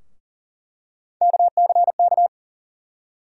Morse code, heard as a keyed tone, about 40 wpm, 700 Hz. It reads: KCK
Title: USA Call Sign Postfixes - 3 Letters 40wpm